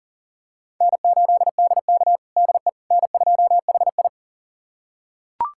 Morse code, heard as a keyed tone, about 40 wpm, 700 Hz. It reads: N8DK DE N2HI
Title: USA Callsign DE Callsign - 4 Characters - 40wpm